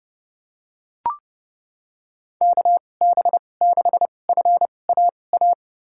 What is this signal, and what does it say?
Morse code, heard as a keyed tone, about 30 wpm, spaced as for 22 wpm, 700 Hz. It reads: KB6FAA
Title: USA Call Signs - 6 Characters - 2x Character Spacing - 30wpm